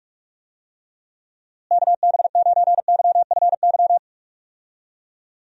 Morse code, heard as a keyed tone, about 46 wpm, 700 Hz. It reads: KB9YFY
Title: USA Call Signs - 6 Characters - Repeated 3x - 45wpm